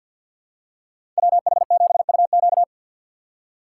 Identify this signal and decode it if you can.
Morse code, heard as a keyed tone, about 53 wpm, 700 Hz. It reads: WH7UQ
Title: USA Call Signs - 5 Characters - 2x Character Spacing - 50wpm